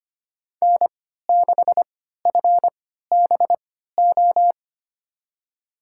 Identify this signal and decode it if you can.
Morse code, heard as a keyed tone, about 25 wpm, spaced as for 14 wpm, 700 Hz. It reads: N6FBO
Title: USA Call Signs - 5 Characters - 3x Character Spacing - 25wpm